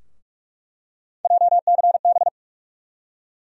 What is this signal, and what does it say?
Morse code, heard as a keyed tone, about 47 wpm, 700 Hz. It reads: JCB
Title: USA Call Sign Postfixes - 3 Letters 45wpm